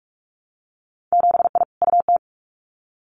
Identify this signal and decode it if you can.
Morse code, heard as a keyed tone, about 46 wpm, 690 Hz. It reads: 7I UT
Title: Winter Field Day Exchanges Encore 45wpm